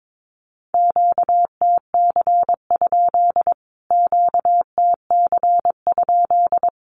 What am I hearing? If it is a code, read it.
Morse code, heard as a keyed tone, about 22 wpm, 700 Hz. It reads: QTC? QTC?
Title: Q-Codes Repeated 3x 22wpm